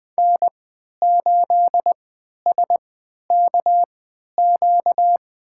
Morse code, heard as a keyed tone, about 20 wpm, spaced as for 11 wpm, 700 Hz. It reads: N8SKQ
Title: USA Call Signs - 5 Characters - 3x Character Spacing - 20wpm